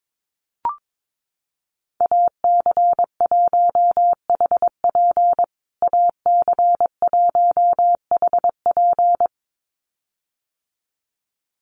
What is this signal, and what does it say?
Morse code, heard as a keyed tone, about 22 wpm, 700 Hz. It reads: AC1HP AC1HP
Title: USA Call Signs Repeated 2x 22wpm